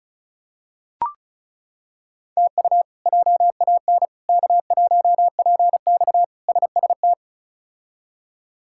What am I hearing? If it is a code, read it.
Morse code, heard as a keyed tone, about 34 wpm, 700 Hz. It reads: TU JAN K1PX SST
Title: K1USN SST QSO Practice - 35wpm